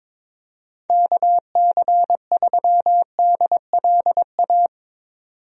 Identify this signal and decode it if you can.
Morse code, heard as a keyed tone, about 22 wpm, 700 Hz. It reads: KC3DLA
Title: USA Call Signs - 6 Characters - 22wpm